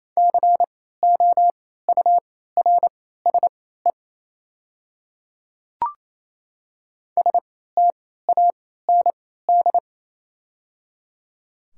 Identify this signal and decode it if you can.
Morse code, heard as a keyed tone, about 28 wpm, spaced as for 16 wpm, 700 Hz. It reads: COURSE STAND
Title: 3x Character Spacing - Top 500 Words - 28wpm